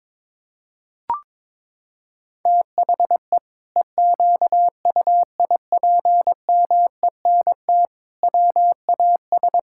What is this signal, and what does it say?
Morse code, heard as a keyed tone, about 21 wpm, 700 Hz. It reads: THE EQUIPMENT WAS
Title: Sentences from Top 1000 Words - Review Part 5 - 22wpm